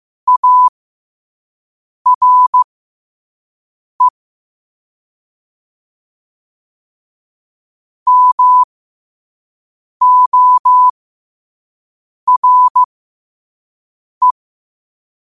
Morse code, heard as a keyed tone, about 15 wpm, spaced as for 5 wpm, 1000 Hz. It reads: ARE MORE
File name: qotdfiles / 5wpm